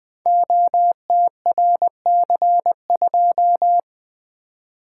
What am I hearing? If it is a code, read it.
Morse code, heard as a keyed tone, about 20 wpm, 700 Hz. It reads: OTRC2
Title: Code Groups - 5 Letter:Numbers - 20wpm